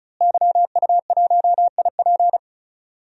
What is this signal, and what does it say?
Morse code, heard as a keyed tone, about 36 wpm, 700 Hz. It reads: YU1IP
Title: DXCC-100 Call Signs - 5 Characters - Repeated 3x - 35wpm